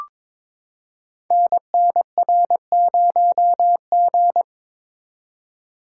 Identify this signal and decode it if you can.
Morse code, heard as a keyed tone, about 22 wpm, 700 Hz. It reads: NNR0G
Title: Code Groups - 5 Letter:Numbers - 22wpm